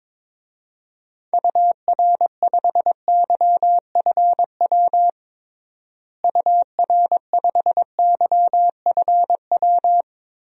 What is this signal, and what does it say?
Morse code, heard as a keyed tone, about 22 wpm, 700 Hz. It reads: UR5YFW UR5YFW
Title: DXCC-100 Call Signs - 6 Characters - Repeated 2x - 22wpm